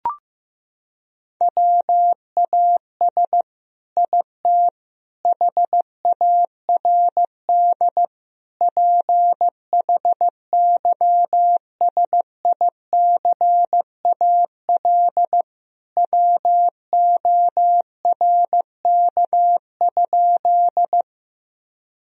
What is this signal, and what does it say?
Morse code, heard as a keyed tone, about 15 wpm, 700 Hz. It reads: WAS IT HARD PHYSICAL WORK?
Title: Sentences from Top 1000 Words - Review Part 4 - 15wpm